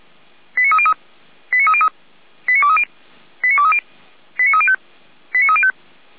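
Five-tone selective call (ZVEI), heard as quick runs of five tones, tones of about 70 ms. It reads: 89303 89303 89230 89230 89305 89305
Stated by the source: ZVEI five-tone selective call, code 89315, received with an SDR, from feuerwehr-orlamuende.org (fire brigade Orlamünde).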